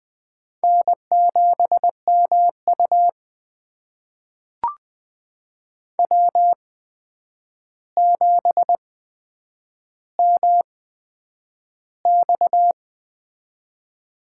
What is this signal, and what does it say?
Morse code, heard as a keyed tone, about 21 wpm, spaced as for 9 wpm, 700 Hz. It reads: N7MU W7MX
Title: USA Call Signs - 4 Characters - 8x Character Spacing - 20wpm